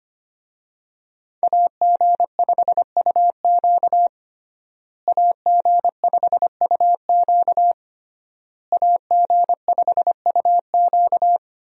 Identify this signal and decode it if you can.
Morse code, heard as a keyed tone, about 25 wpm, 700 Hz. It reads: AG5UQ AG5UQ AG5UQ
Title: USA Call Signs - 5 Characters - Repeated 3x - 25wpm